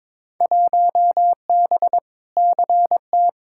Morse code, heard as a keyed tone, about 22 wpm, 700 Hz. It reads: 1B CT
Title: Field Day Exchanges Encore - 22wpm